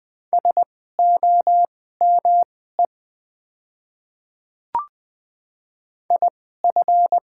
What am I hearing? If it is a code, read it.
Morse code, heard as a keyed tone, about 20 wpm, spaced as for 14 wpm, 700 Hz. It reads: SOME IF
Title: 2x Character Spacing - Top 200 Words - 20wpm